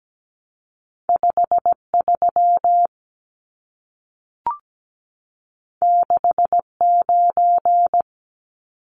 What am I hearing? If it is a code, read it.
Morse code, heard as a keyed tone, about 17 wpm, 700 Hz. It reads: 53 69
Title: Numbers - 2-Digit 17wpm